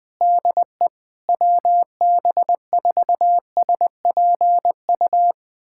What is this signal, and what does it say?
Morse code, heard as a keyed tone, about 20 wpm, 700 Hz. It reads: DE WB4SPU
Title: USA Callsign DE Callsign - 6 Characters - 20wpm